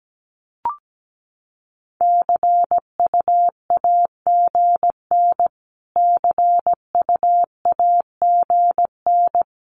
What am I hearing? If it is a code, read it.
Morse code, heard as a keyed tone, about 17 wpm, 700 Hz. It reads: CUAGN CUAGN
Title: Abbreviations Repeated 3x - 17wpm